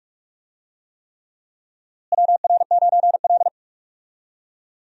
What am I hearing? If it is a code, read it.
Morse code, heard as a keyed tone, about 46 wpm, 700 Hz. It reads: WR9L
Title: USA Call Signs - 4 Characters - Repeated 3x - 45wpm